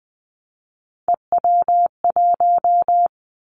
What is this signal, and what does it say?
Morse code, heard as a keyed tone, about 20 wpm, 700 Hz. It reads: EW1
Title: Callsign Prefixes for Top 100 DXCC - 20wpm